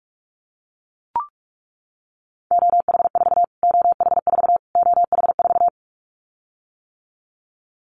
Morse code, heard as a keyed tone, about 44 wpm, 700 Hz. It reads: OH4 OH4 OH4
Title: Callsign Prefixes for Top 25 DXCC - Repeated 3x - 45wpm